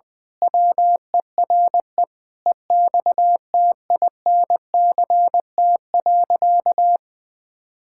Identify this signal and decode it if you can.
Morse code, heard as a keyed tone, about 20 wpm, 700 Hz. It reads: WERE EXTINCT.
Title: Pangram Challenge v2 - 20wpm